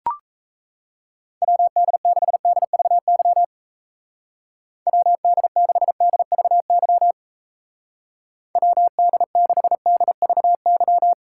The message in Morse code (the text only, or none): WD6DVY WD6DVY WD6DVY